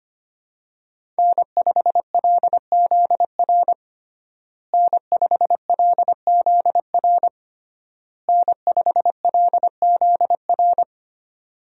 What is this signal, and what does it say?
Morse code, heard as a keyed tone, about 25 wpm, 700 Hz. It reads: N5LZR N5LZR N5LZR
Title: USA Call Signs - 5 Characters - Repeated 3x - 25wpm